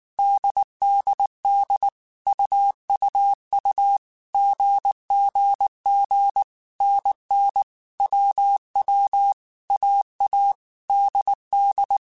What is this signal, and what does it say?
Morse code, heard as a keyed tone, about 19 wpm, 800 Hz. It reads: DDD UUU GGG NN WW AA DD